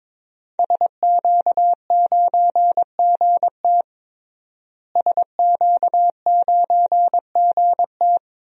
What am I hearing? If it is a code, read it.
Morse code, heard as a keyed tone, about 22 wpm, 700 Hz. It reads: SQ9GT SQ9GT
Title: DXCC-100 Call Signs - 5 Characters - Repeated 3x - 22wpm